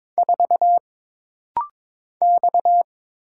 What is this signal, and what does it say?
Morse code, heard as a keyed tone, about 22 wpm, 700 Hz. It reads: 4 X
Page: This practice set is composed of Letters, Numbers, common punctuations, and common prosigns.